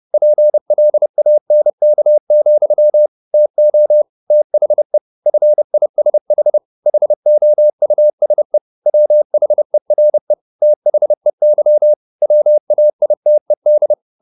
Morse code, heard as a keyed tone, about 30 wpm, 600 Hz. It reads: PLANK, TO THE FISH HOUSE WHERE THEY WAITED
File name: the old man and the sea / the old man and the sea 1 30wpm